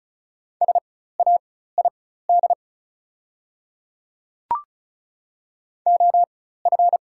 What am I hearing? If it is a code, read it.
Morse code, heard as a keyed tone, about 36 wpm, spaced as for 16 wpm, 690 Hz. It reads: SAID OF
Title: 4x Character Spacing - Top 100 Words - 35wpm